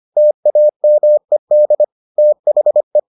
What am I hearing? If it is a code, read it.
Morse code, heard as a keyed tone, about 25 wpm, 600 Hz. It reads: TAMED THE